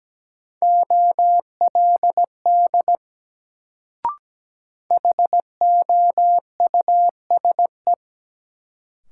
Morse code, heard as a keyed tone, about 17 wpm, 700 Hz. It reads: OLD HOUSE
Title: Sets of 2 Words - New Word Drill - 17wpm